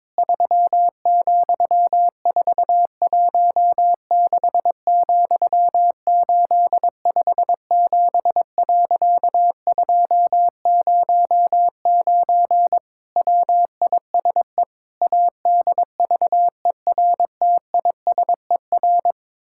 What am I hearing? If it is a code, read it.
Morse code, heard as a keyed tone, about 22 wpm, 700 Hz. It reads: 3,416,857.209 WISE ADVERTISER
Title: Pangram Challenge v2 - 22wpm